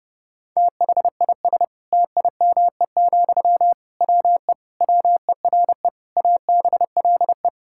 Morse code, heard as a keyed tone, about 30 wpm, 700 Hz. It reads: THIS TIME, WE WERE ABLE
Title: Sentences from Top 1000 Words - Review Part 6 - 30wpm